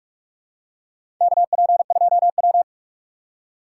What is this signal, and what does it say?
Morse code, heard as a keyed tone, about 46 wpm, 700 Hz. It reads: KP2W
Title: USA Call Signs - 4 Characters - 2x Character Spacing - 45wpm